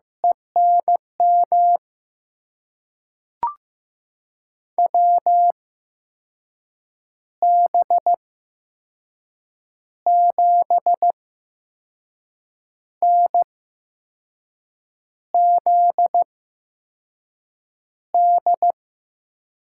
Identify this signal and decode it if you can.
Morse code, heard as a keyed tone, about 15 wpm, spaced as for 5 wpm, 700 Hz. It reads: ENM WB7NZD
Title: USA Call Signs - 6 Characters - 8x Character Spacing - 15wpm